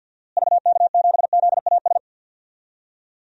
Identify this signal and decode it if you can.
Morse code, heard as a keyed tone, about 49 wpm, 700 Hz. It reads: VK7ZAS
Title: DXCC-100 Call Signs - 6 Characters - Repeated 2x - 50wpm